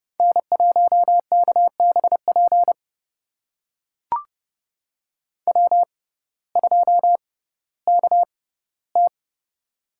Morse code, heard as a keyed tone, about 31 wpm, spaced as for 17 wpm, 700 Hz. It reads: N1KBP W2KT
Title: USA Call Signs - 6x Character Spacing - 30wpm